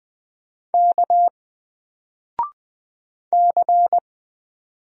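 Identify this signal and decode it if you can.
Morse code, heard as a keyed tone, about 20 wpm, 700 Hz. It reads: K C
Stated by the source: Single letter and number Morse drill at 20 wpm, from N4LCD.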